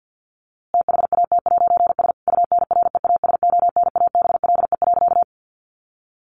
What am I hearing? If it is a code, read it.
Morse code, heard as a keyed tone, about 50 wpm, 700 Hz. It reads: THAT'S UNREASONABLE.
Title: A Sound Of Thunder 50wpm